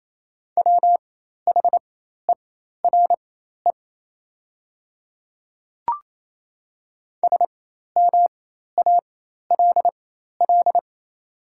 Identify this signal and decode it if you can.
Morse code, heard as a keyed tone, about 28 wpm, spaced as for 13 wpm, 700 Hz. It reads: WHERE SMALL
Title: 4x Character Spacing - Top 300 Words - 28wpm